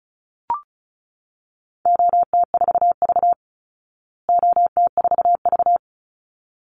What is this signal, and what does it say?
Morse code, heard as a keyed tone, about 34 wpm, 700 Hz. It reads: OT4V OT4V
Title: Call Signs Repeated 3x - 35wpm